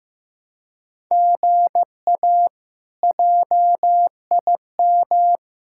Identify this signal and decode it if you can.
Morse code, heard as a keyed tone, about 15 wpm, 700 Hz. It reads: GA JIM
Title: K1USN SST QSO Practice - 15wpm